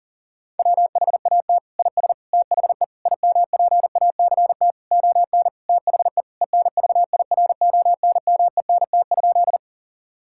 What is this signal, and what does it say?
Morse code, heard as a keyed tone, about 37 wpm, 700 Hz. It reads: WHAT IS THE IMPACT ON THE ENVIRONMENT?
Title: Sentences from Top 1000 Words - Review Part 6 - 40wpm